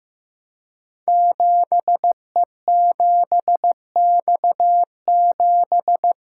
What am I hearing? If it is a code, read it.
Morse code, heard as a keyed tone, about 15 wpm, 700 Hz. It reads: 7E7X7
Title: Code Groups - 5 Letter:Numbers - 15wpm